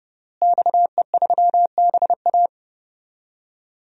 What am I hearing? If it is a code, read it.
Morse code, heard as a keyed tone, about 30 wpm, 700 Hz. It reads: XE3BA
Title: DXCC-100 Call Signs - 5 Characters - Repeated 3x - 30wpm